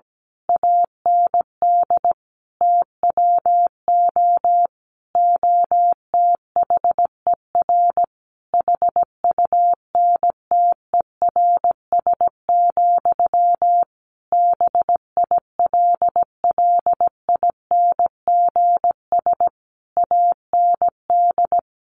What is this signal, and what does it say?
Morse code, heard as a keyed tone, about 17 wpm, 700 Hz. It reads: AND TWO OTHER HUNTERS, BILLINGS AND
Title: A Sound Of Thunder 17wpm